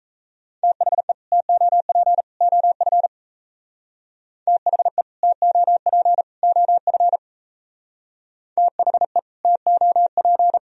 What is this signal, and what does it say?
Morse code, heard as a keyed tone, about 37 wpm, 700 Hz. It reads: THE TOP OF THE TOP OF THE TOP